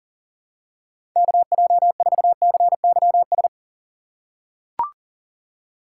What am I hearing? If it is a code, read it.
Morse code, heard as a keyed tone, about 40 wpm, 690 Hz. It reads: KJ4CYS
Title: USA Call Signs - 4x Character Spacing - 40wpm